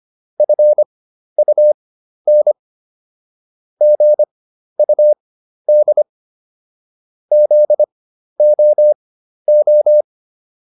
Morse code, heard as a keyed tone, about 25 wpm, spaced as for 12 wpm, 600 Hz. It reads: FUN GUD ZOO